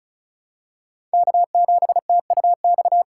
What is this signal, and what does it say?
Morse code, heard as a keyed tone, about 36 wpm, 700 Hz. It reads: K7TUX
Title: USA Call Signs - 5 Characters - 4x Character Spacing - 35wpm